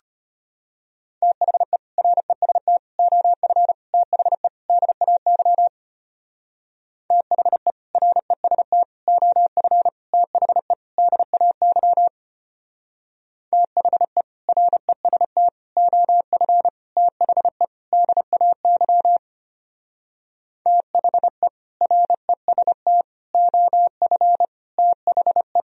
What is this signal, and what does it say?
Morse code, heard as a keyed tone, about 31 wpm, 700 Hz. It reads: THE REST OF THE DAY THE REST OF THE DAY THE REST OF THE DAY THE REST OF THE